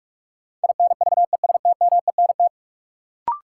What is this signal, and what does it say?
Morse code, heard as a keyed tone, about 45 wpm, 700 Hz. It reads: INVESTMENT
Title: Sentences from Top 700 Words - New Word Drill - 45wpm